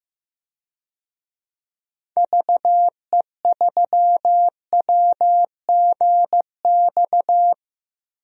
Morse code, heard as a keyed tone, about 15 wpm, 700 Hz. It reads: VE3WGX
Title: DXCC-100 Call Signs - 6 Characters - 15wpm